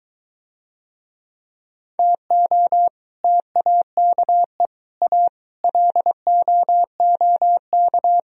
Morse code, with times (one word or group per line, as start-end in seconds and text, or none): 2.0-2.9 TO
3.2-4.7 TAKE
5.0-5.3 A
5.6-8.2 LOOK